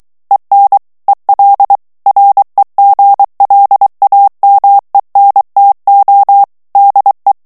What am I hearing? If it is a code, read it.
Morse code, heard as a keyed tone, about 23 wpm, 800 Hz. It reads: EN EL REGLAMENTO DE
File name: morse mp3 / ESPAGNOL TEXTE EN CLAIR 2 VITESSE 22 MOTS MINUTES